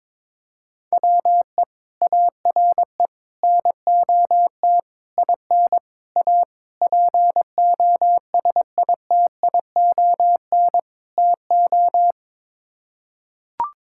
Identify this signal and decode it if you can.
Morse code, heard as a keyed tone, about 22 wpm, 700 Hz. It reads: WE ARE NOT IN A POSITION TO
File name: practice / 22wpm / Sets of 7 Words - 22wpm